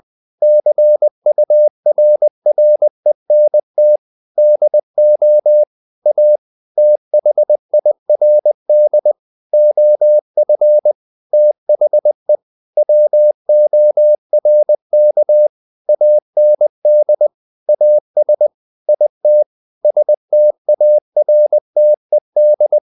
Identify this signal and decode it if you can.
Morse code, heard as a keyed tone, about 20 wpm, 600 Hz. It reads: CURRENT DO A THIRD OF THE WORK AND AS IT STARTED
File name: the old man and the sea / the old man and the sea part 1 20wpm